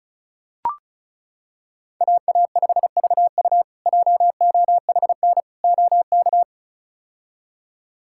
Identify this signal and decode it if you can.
Morse code, heard as a keyed tone, about 36 wpm, 700 Hz. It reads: AA5VU JOHN OK
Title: K1USN SST QSO Practice - 35wpm